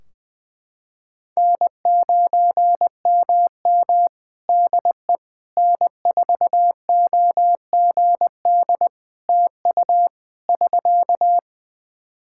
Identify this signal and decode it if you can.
Morse code, heard as a keyed tone, about 20 wpm, 700 Hz. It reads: N9MM DE N4OGD TU <SK>
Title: K1USN SST QSO Practice 20wpm